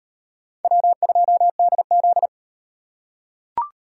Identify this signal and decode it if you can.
Morse code, heard as a keyed tone, about 38 wpm, 700 Hz. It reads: W2DZ